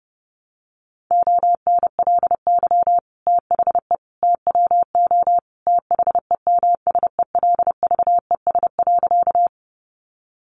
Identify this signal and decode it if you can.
Morse code, heard as a keyed tone, about 30 wpm, 700 Hz. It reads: ONLY THE TWO THEMSELVES.
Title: Sentences from Top 800 Words 30wpm